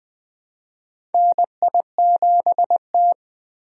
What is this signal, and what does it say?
Morse code, heard as a keyed tone, about 20 wpm, 700 Hz. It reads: NI7T
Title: USA Call Signs - 4 Characters - Repeated 3x - 20wpm